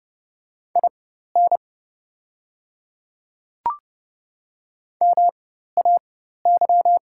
Morse code, heard as a keyed tone, about 30 wpm, spaced as for 14 wpm, 700 Hz. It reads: IN MAY
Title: 4x Character Spacing - Top 100 Words - 30wpm